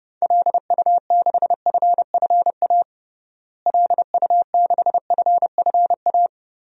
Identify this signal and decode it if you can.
Morse code, heard as a keyed tone, about 30 wpm, 700 Hz. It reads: LU6FFA LU6FFA
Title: DXCC-100 Call Signs - 6 Characters - Repeated 3x - 30wpm